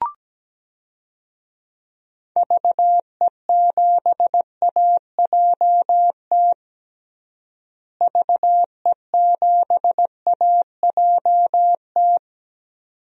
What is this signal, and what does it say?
Morse code, heard as a keyed tone, about 17 wpm, 700 Hz. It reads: VE7AJT VE7AJT
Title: DXCC-100 Call Signs - 6 Characters - Repeated 2x - 17wpm